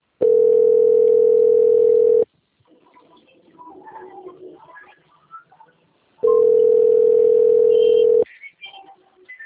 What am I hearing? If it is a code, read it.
Ringback tone (North America), 440 + 480 Hz.
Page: Vila Isabel - RJ